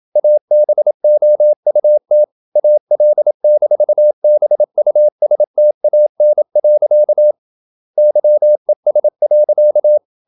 Morse code, heard as a keyed tone, about 27 wpm, 600 Hz. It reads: ABOUT AL-BUSTAN. YES.
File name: Never 0027wpm